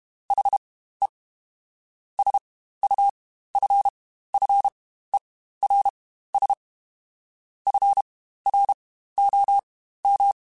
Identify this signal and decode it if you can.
Morse code, heard as a keyed tone, about 32 wpm, spaced as for 15 wpm, 800 Hz. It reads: HE SUFFERS FROM
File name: Quotes Spaced / Quotes Spaced-mp3s / HoofInMouth sp @30